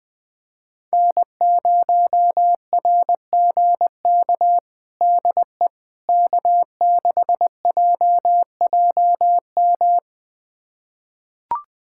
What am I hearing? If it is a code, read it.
Morse code, heard as a keyed tone, about 20 wpm, 700 Hz. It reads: N0RGK DE K6JJM
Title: USA Callsign DE Callsign - 5 Characters - 20wpm